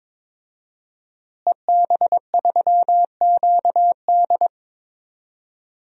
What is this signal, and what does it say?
Morse code, heard as a keyed tone, about 22 wpm, 700 Hz. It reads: EB3QD
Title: DXCC-100 Call Signs - 5 Characters - Repeated 2x - 22wpm